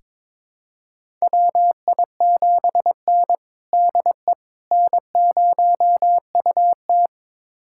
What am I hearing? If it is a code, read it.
Morse code, heard as a keyed tone, about 21 wpm, 700 Hz. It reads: WI7N DE N0UT
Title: USA Callsign DE Callsign - 4 Characters - 22wpm